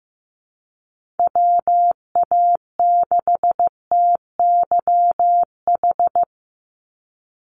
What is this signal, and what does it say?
Morse code, heard as a keyed tone, about 15 wpm, 700 Hz. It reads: WA6TYH
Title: USA Call Signs 15wpm